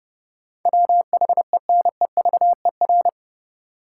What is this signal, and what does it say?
Morse code, heard as a keyed tone, about 30 wpm, 700 Hz. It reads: WHENEVER
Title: Sentences from Top 1200 Words - New Word Drill - 30wpm